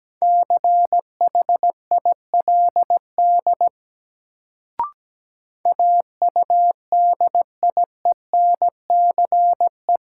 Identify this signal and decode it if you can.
Morse code, heard as a keyed tone, about 17 wpm, 700 Hz. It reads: CHILD AUDIENCE
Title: Sentences from Top 800 Words - New Word Drill - 17wpm